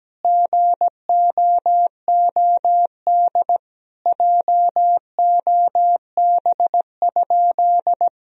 Morse code, heard as a keyed tone, about 17 wpm, 700 Hz. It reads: GOOD JOB?
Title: Sentences from Top 1000 Words - Review Part 3 - 17wpm